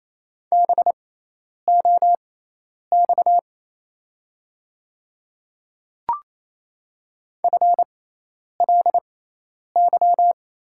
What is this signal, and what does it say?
Morse code, heard as a keyed tone, about 28 wpm, spaced as for 10 wpm, 700 Hz. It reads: BOX FLY
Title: 6x Character Spacing - Top 500 Words - 28wpm